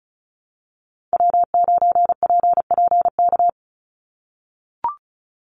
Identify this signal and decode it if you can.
Morse code, heard as a keyed tone, about 34 wpm, 700 Hz. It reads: W9PPK
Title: USA Call Signs 35wpm